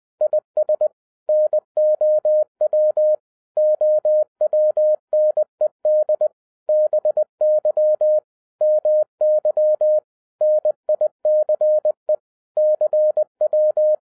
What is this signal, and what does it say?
Morse code, heard as a keyed tone, about 20 wpm, 610 Hz. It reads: IS NOW OWNED BY MY NICE CW
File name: 2011-07-mp3 / n58 BACK FROM A BIG ACCIDENT 20wpm